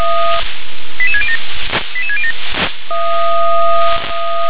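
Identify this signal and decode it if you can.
Five-tone selective call (ZVEI), heard as quick runs of five tones, tones of about 70 ms. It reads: 88607 88607
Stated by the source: ZVEI five-tone selective call received by SDR.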